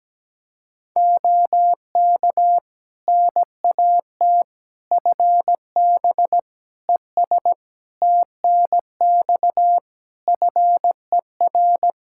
Morse code, heard as a keyed tone, about 17 wpm, 700 Hz. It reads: OK NAT FB ES TNX FER
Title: Practice QSO - Exchange 2 - Info - 17wpm